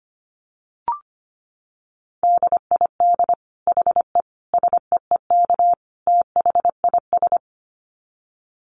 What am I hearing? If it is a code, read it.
Morse code, heard as a keyed tone, about 25 wpm, 700 Hz. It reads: DID HE SEEK THIS
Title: Dit Challenge 25wpm